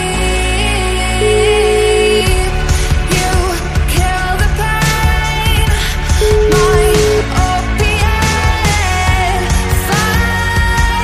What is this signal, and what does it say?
Ringback tone (Europe, 425 Hz), with noise.